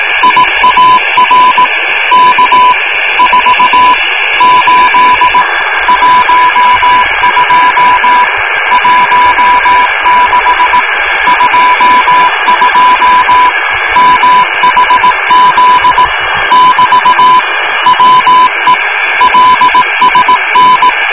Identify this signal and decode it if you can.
Morse code, heard as a keyed tone, about 18 wpm, 1000 Hz, with radio noise. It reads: IAR K 4 8 12 16 22 MHZ <BT> WE LSN